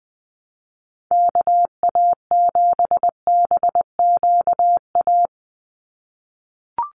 Morse code, heard as a keyed tone, about 20 wpm, 700 Hz. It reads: KA7BQA
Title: USA Call Signs 20wpm